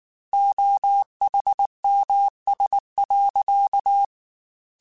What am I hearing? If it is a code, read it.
Morse code, heard as a keyed tone, about 19 wpm, 800 Hz. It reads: OHMS.